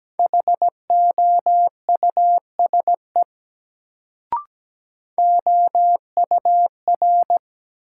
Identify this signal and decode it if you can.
Morse code, heard as a keyed tone, about 17 wpm, 700 Hz. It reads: HOUSE OUR
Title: Sentences from Top 200 Words - New Word Drill - 17wpm